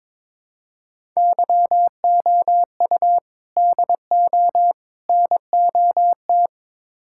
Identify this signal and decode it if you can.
Morse code, heard as a keyed tone, about 22 wpm, 700 Hz. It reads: YOU DO NOT
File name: practice / 22wpm / Build A Phrase - 5-Word Phrases - 22wpm